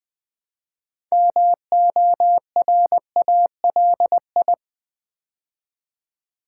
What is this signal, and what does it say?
Morse code, heard as a keyed tone, about 20 wpm, 700 Hz. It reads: MORALI
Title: Build A Word - 3 Letters At A Time - 8-Letter Words - 20wpm